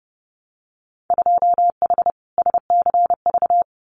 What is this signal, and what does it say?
Morse code, heard as a keyed tone, about 30 wpm, 700 Hz. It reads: 2H SCV
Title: Winter Field Day Exchanges Encore 30wpm